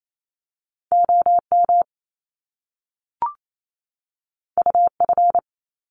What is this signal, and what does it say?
Morse code, heard as a keyed tone, about 28 wpm, 700 Hz. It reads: OM UF
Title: Top Two Letter Combinations in Words 28wpm